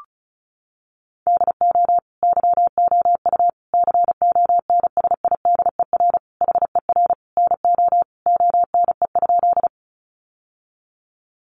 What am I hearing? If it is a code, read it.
Morse code, heard as a keyed tone, about 34 wpm, 700 Hz. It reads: DO YOU CONSIDER HER NO ONE?
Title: Sentences from Top 600 Words 35wpm